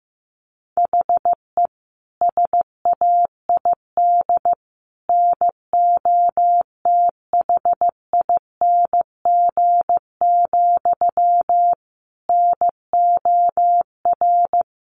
Morse code, heard as a keyed tone, about 14 wpm, 700 Hz. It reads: HE SAID NOTHING, NOR
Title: Sentences from Top 1500 Words 15wpm